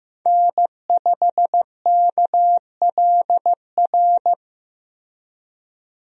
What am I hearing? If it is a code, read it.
Morse code, heard as a keyed tone, about 15 wpm, 700 Hz. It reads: N5KLR
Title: USA Call Signs - 5 Characters - Repeated 3x - 15wpm